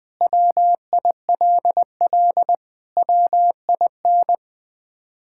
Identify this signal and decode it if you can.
Morse code, heard as a keyed tone, about 20 wpm, 700 Hz. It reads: WILL WIN
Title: Sentences with 3-Word Phrases - Spoken Beginning - 20wpm